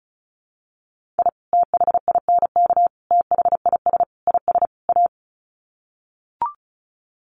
Morse code, heard as a keyed tone, about 34 wpm, 700 Hz. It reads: I THINK THIS IS A
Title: Sets of 5 Words 35wpm